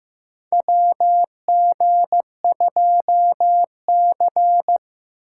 Morse code, heard as a keyed tone, about 15 wpm, 700 Hz. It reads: WG2C
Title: USA Call Signs - 4 Characters - 6x Character Spacing - 15wpm